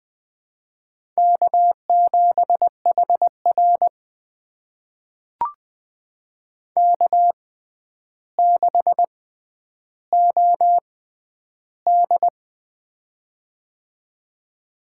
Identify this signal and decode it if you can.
Morse code, heard as a keyed tone, about 20 wpm, spaced as for 10 wpm, 700 Hz. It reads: K7HR K6OD
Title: USA Call Signs - 4 Characters - 6x Character Spacing - 20wpm